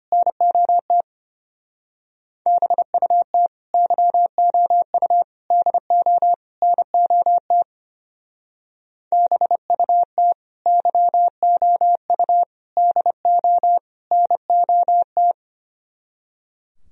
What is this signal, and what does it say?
Morse code, heard as a keyed tone, about 31 wpm, 700 Hz. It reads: NOT BUT YOU DO NOT BUT YOU DO NOT
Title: Speed-Racing - Sets of 4 Words 25wpm